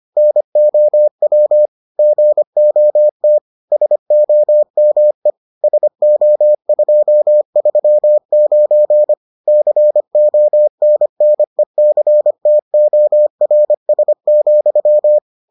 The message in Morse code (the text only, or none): NOW GOT SOME SO239 CONNECTORS,